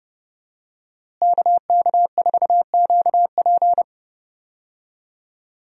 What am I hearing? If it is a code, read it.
Morse code, heard as a keyed tone, about 32 wpm, 700 Hz. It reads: KK4QP
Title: USA Call Signs - 5 Characters - Repeated 3x - 30wpm